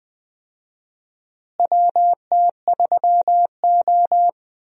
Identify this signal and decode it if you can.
Morse code, heard as a keyed tone, about 20 wpm, 700 Hz. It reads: WT3O
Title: USA Call Signs - 4 Characters - Repeated 2x - 20wpm